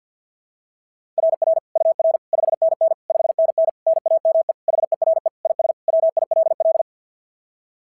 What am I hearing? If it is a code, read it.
Morse code, heard as a keyed tone, about 47 wpm, 690 Hz. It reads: RR UR 5NN 5NN NAME HERE IS WILL
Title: SOTA QSO Practice 50wpm